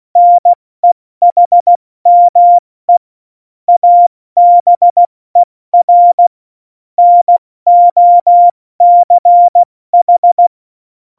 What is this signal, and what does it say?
Morse code, heard as a keyed tone, about 16 wpm, 700 Hz. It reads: NEHME ABER NOCH
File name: Morsekurs / Witze / Witz 01. Oktober 2019, 70 F80